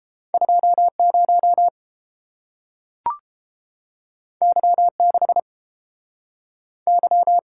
Morse code, heard as a keyed tone, about 30 wpm, 700 Hz. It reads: 20 Y6 Y